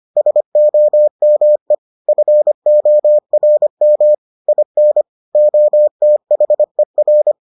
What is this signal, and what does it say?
Morse code, heard as a keyed tone, about 25 wpm, 600 Hz. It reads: SOME FORM IN OTHER